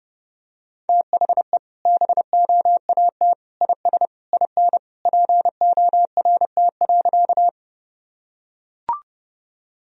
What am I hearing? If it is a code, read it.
Morse code, heard as a keyed tone, about 30 wpm, 700 Hz. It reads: THE BOAT IS IN PORT.
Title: Sentences from Top 1000 Words - Review Part 1 - 30wpm